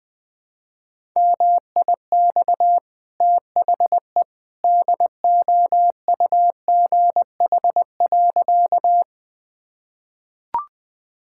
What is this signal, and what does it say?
Morse code, heard as a keyed tone, about 21 wpm, 700 Hz. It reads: MIX THE DOUGH.
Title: Easy Sentences with Hints - 20wpm